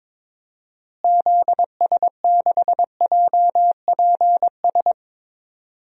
Morse code, heard as a keyed tone, about 22 wpm, 700 Hz. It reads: ZS6JPS
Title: DXCC-100 Call Signs - 6 Characters - Repeated 3x - 22wpm